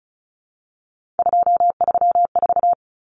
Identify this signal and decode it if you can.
Morse code, heard as a keyed tone, about 34 wpm, 700 Hz. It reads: 234
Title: Numbers - 3-Digit 35wpm